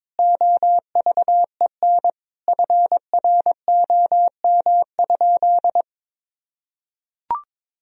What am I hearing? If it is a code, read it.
Morse code, heard as a keyed tone, about 22 wpm, 700 Hz. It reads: OVEN FROM?
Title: Sentences from Top 1000 Words - Review Part 4 - 22wpm